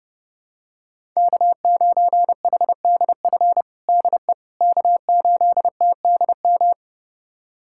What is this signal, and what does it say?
Morse code, heard as a keyed tone, about 30 wpm, 700 Hz. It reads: K9HDF DE K8TDM
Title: USA Callsign DE Callsign - 5 Characters - 30wpm